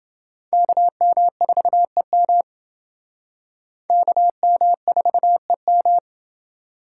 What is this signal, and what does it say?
Morse code, heard as a keyed tone, about 29 wpm, 700 Hz. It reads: KM4EM KM4EM